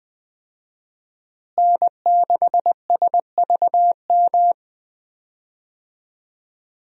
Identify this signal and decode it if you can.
Morse code, heard as a keyed tone, about 21 wpm, 700 Hz. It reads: N6SVM
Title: USA Call Signs - 5 Characters - Repeated 3x - 20wpm